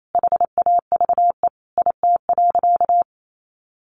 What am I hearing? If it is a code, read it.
Morse code, heard as a keyed tone, about 28 wpm, 700 Hz. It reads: HAVE IT.
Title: Sentences from Top 100 Words 28wpm